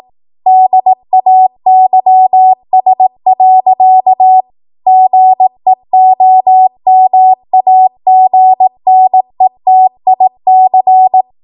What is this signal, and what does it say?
Morse code, heard as a keyed tone, about 18 wpm, 750 Hz. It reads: DAYS. GEOMAGNETIC
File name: Morse / 18WPM